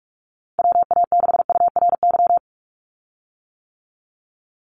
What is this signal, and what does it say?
Morse code, heard as a keyed tone, about 46 wpm, 700 Hz. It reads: WA6URY
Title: Call Signs - 45wpm